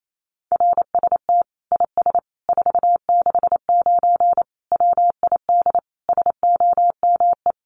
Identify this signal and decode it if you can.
Morse code, heard as a keyed tone, about 28 wpm, 700 Hz. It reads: RST IS 469 WID SOME
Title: Common QSO Phrases 28wpm